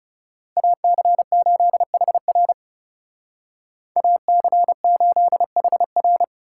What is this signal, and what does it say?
Morse code, heard as a keyed tone, about 37 wpm, 700 Hz. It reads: AC8HR AC8HR
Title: Speed Racing - USA 2x2 Call Signs 30wpm